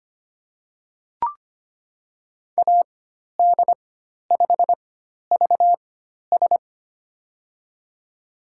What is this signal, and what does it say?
Morse code, heard as a keyed tone, about 25 wpm, 700 Hz. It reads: A D 5 V S
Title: USA Call Signs - 4x Character Spacing - 25wpm